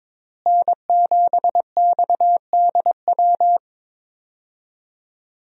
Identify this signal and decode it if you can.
Morse code, heard as a keyed tone, about 22 wpm, 700 Hz. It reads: N7XDW
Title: USA Call Signs - 5 Characters - Repeated 3x - 22wpm